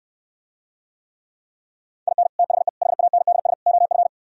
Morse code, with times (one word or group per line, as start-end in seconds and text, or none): 2.1-2.3 IN
2.4-2.7 THE
2.8-3.5 HANDS
3.7-4.1 OF